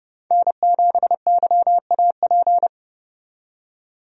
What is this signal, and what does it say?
Morse code, heard as a keyed tone, about 30 wpm, 700 Hz. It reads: N7YAP
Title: USA Call Signs - 5 Characters - 30wpm